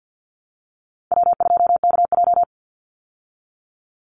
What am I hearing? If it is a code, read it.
Morse code, heard as a keyed tone, about 50 wpm, 700 Hz. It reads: W2KJ
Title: Call Signs Repeated 3x - 50wpm